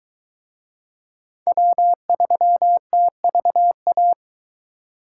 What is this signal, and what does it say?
Morse code, heard as a keyed tone, about 20 wpm, 700 Hz. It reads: W3TVA